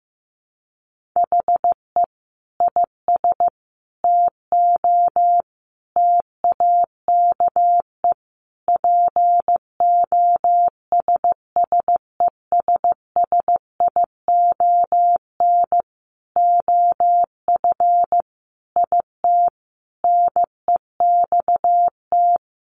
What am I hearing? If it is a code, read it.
Morse code, heard as a keyed tone, about 15 wpm, 700 Hz. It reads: HE IS TO TAKE POSSESSION OF IT NEXT
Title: Sentences from Top 1400 Words 15wpm